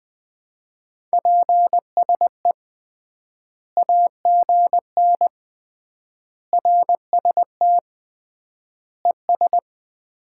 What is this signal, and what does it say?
Morse code, heard as a keyed tone, about 21 wpm, 700 Hz. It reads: PSE AGN RST ES
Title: Common QSO Phrases - 3x Word-Spacing - 20wpm